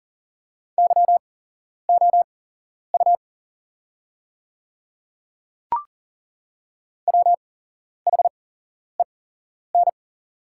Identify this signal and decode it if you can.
Morse code, heard as a keyed tone, about 40 wpm, spaced as for 11 wpm, 700 Hz. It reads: YOU WHEN
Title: 8x Character Spacing - Top 200 Words - 40wpm